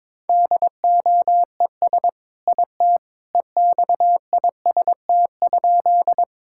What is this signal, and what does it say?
Morse code, heard as a keyed tone, about 22 wpm, 700 Hz. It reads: DOES IT EXIST?
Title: Sentences from Top 1000 Words - Review Part 5 - 22wpm